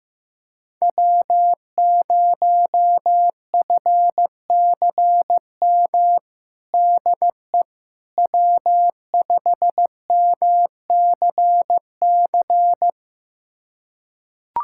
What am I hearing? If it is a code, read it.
Morse code, heard as a keyed tone, about 16 wpm, 700 Hz. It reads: W0FCM DE W5MCC
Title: USA Callsign DE Callsign - 5 Characters - 15wpm